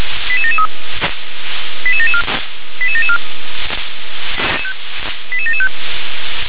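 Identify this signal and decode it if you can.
Five-tone selective call (ZVEI), heard as quick runs of five tones, tones of about 70 ms. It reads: 88773 88774 88774 88775
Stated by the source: ZVEI five-tone selective call received by SDR.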